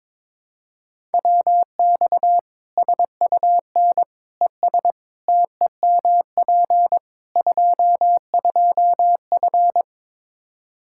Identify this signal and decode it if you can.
Morse code, heard as a keyed tone, about 21 wpm, 700 Hz. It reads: WX SUN ES TEMP 22F
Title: Practice QSO - Exchange 2 - Info - 22wpm